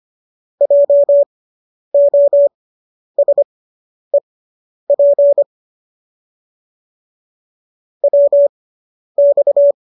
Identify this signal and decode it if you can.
Morse code, heard as a keyed tone, about 25 wpm, spaced as for 10 wpm, 570 Hz. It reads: JOSEP WX